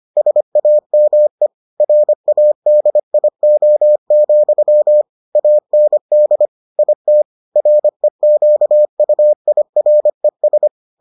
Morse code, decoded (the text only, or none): SAME RADIO, AND IT REQUIRES